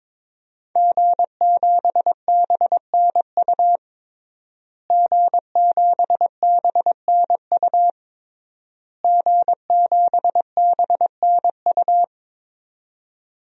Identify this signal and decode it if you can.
Morse code, heard as a keyed tone, about 22 wpm, 700 Hz. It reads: G7BNU G7BNU G7BNU
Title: DXCC-100 Call Signs - 5 Characters - Repeated 3x - 22wpm